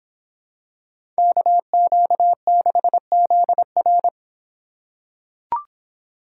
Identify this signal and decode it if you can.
Morse code, heard as a keyed tone, about 26 wpm, 700 Hz. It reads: KQ6ZR